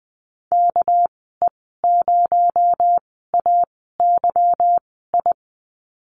Morse code, heard as a keyed tone, about 20 wpm, spaced as for 14 wpm, 700 Hz. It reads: KE0AYI
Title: 245 Lesson 39g - Letter z - Call Signs 20wpm